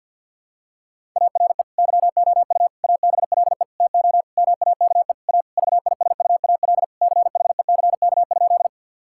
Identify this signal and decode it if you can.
Morse code, heard as a keyed tone, about 50 wpm, 700 Hz. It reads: ARE YOU ABLE TO MAKE A VISUAL CHECK?
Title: Sentences from Top 1000 Words - Review Part 6 - 50wpm